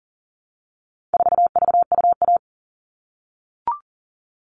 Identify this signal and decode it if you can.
Morse code, heard as a keyed tone, about 40 wpm, 700 Hz. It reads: 4VUA
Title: All Cut Numbers - 4-Digit 40wpm